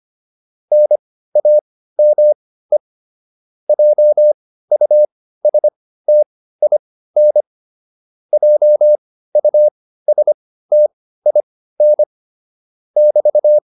Morse code, heard as a keyed tone, about 25 wpm, spaced as for 15 wpm, 600 Hz. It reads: NAME JUSTIN JUSTIN <BT>